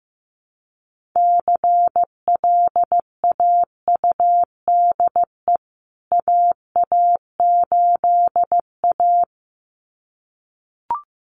Morse code, heard as a keyed tone, about 15 wpm, 700 Hz. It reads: CLAUDE AA8A
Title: CWT Exchange 15wpm